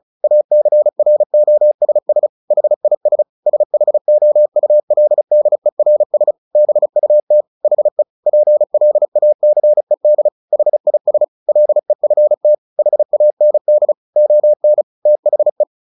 Morse code, heard as a keyed tone, about 34 wpm, 600 Hz. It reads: ACROSS HIS SHOULDERS BUT HE PLACED HIS LEFT HAND ON THE
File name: the old man and the sea 2 35wpm